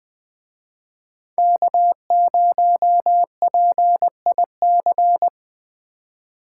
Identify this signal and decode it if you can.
Morse code, heard as a keyed tone, about 20 wpm, 700 Hz. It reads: K0PIC